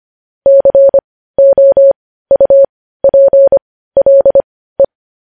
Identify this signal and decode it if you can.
Morse code, heard as a keyed tone, about 25 wpm, spaced as for 15 wpm, 550 Hz. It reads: COUPLE
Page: news-headlines-in-morse-code-at-15-wpm